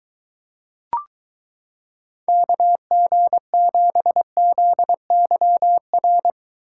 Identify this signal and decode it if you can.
Morse code, heard as a keyed tone, about 23 wpm, 700 Hz. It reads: KG7ZYR